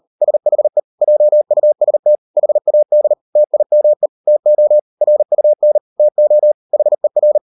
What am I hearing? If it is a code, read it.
Morse code, heard as a keyed tone, about 40 wpm, 600 Hz. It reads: SHE JUST HAD TIME TO RUN TO HER